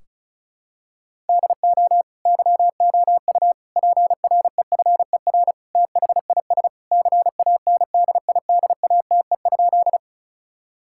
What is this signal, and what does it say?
Morse code, heard as a keyed tone, about 36 wpm, 700 Hz. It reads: DO YOU PREFER THIS CANDIDATE?
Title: Sentences from Top 1600 Words 35wpm